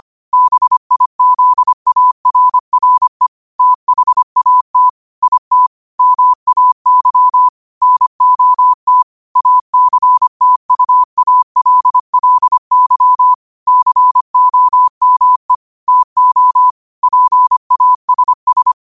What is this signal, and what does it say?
Morse code, heard as a keyed tone, about 25 wpm, 1000 Hz. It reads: BIZARRE THAT IT MAY NOT ACTUALLY COME TO PASS
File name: qotdfiles / 25wpm